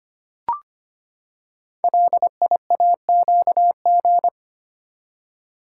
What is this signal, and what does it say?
Morse code, heard as a keyed tone, about 25 wpm, 700 Hz. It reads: LIAQG
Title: Code Groups - 5 Letters - 25wpm